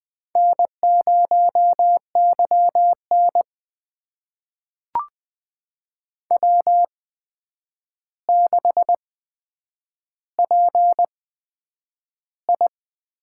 Morse code, heard as a keyed tone, about 20 wpm, spaced as for 9 wpm, 700 Hz. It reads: N0YN W6PI
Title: USA Call Signs - 4 Characters - 8x Character Spacing - 20wpm